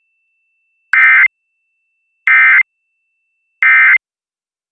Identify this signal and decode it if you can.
Emergency Alert System (SAME) tones, heard as harsh buzzing bursts. It is heard from 0.9 to 4.0 s.